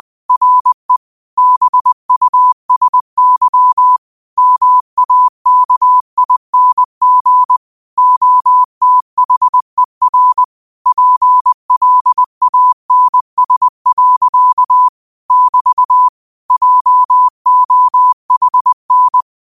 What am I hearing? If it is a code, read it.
Morse code, heard as a keyed tone, about 20 wpm, 1000 Hz. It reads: RE BUSY MAKING OTHER PLANS. <BT> JOHN
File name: qotdfiles / 20wpm